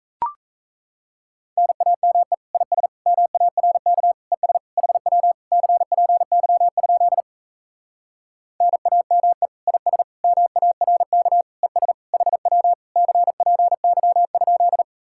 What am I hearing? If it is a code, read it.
Morse code, heard as a keyed tone, about 39 wpm, 700 Hz. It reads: NAME IS MARK ES HW CPY? NAME IS MARK ES HW CPY?